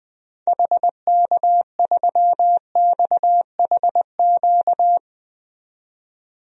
Morse code, heard as a keyed tone, about 20 wpm, 700 Hz. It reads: HK3XHQ
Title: DXCC-100 Call Signs - 6 Characters - 20wpm